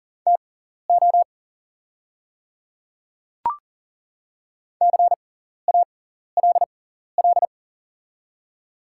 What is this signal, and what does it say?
Morse code, heard as a keyed tone, about 40 wpm, spaced as for 14 wpm, 700 Hz. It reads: TO CALL
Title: 6x Character Spacing - Top 100 Words - 40wpm